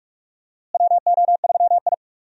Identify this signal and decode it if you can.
Morse code, heard as a keyed tone, about 46 wpm, 700 Hz. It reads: WO3I
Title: USA Call Signs - 4 Characters - Repeated 3x - 45wpm